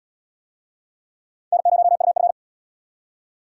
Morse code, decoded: N0UF

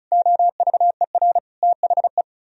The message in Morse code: OVER THE

